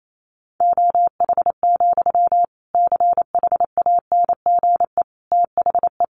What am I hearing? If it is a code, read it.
Morse code, heard as a keyed tone, about 28 wpm, 700 Hz. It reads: OH, CHANGE THE
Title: Sentences from Top 800 Words 28wpm